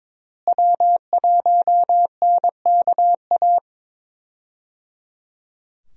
Morse code, heard as a keyed tone, about 22 wpm, 700 Hz. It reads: W1NKA
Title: USA Call Signs - 5 Characters - Repeated 3x - 22wpm